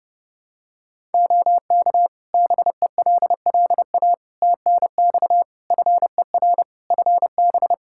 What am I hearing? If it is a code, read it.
Morse code, heard as a keyed tone, about 30 wpm, 700 Hz. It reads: OK BELLA TNX FER FB
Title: K1USN SST QSO Practice 30wpm